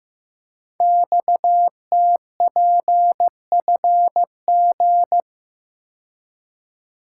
Morse code, heard as a keyed tone, about 15 wpm, 700 Hz. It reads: XTPFG
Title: Code Groups - 5 Letters - 15wpm